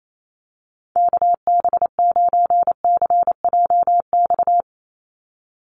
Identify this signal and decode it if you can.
Morse code, heard as a keyed tone, about 28 wpm, 700 Hz. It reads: KB9CJX
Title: USA Call Signs 28wpm